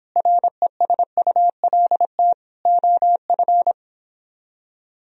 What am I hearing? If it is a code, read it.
Morse code, heard as a keyed tone, about 29 wpm, 700 Hz. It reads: RESULT OF